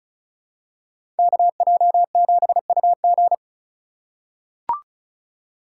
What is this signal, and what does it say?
Morse code, heard as a keyed tone, about 36 wpm, 700 Hz. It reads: KJ7UG
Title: USA Call Signs - 5 Characters - 3x Character Spacing - 35wpm